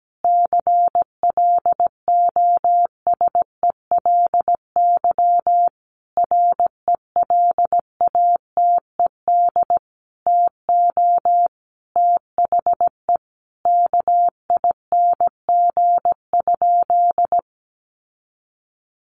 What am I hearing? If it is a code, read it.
Morse code, heard as a keyed tone, about 17 wpm, 700 Hz. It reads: CLOSELY RELATED TO THE KING?
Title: Sentences from Top 1400 Words 17wpm